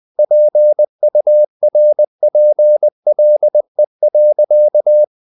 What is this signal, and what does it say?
Morse code, heard as a keyed tone, about 20 wpm, 600 Hz. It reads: PURPLE.
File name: the old man and the sea / the old man and the sea part 1 20wpm